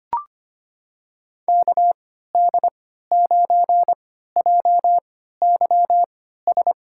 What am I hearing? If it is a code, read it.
Morse code, heard as a keyed tone, about 25 wpm, spaced as for 14 wpm, 700 Hz. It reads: KD9JYS
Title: USA Call Signs - 6 Characters - 3x Character Spacing - 25wpm